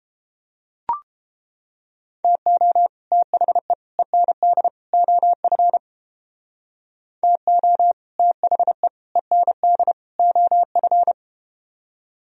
Morse code, heard as a keyed tone, about 31 wpm, 700 Hz. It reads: TO THE END OF TO THE END OF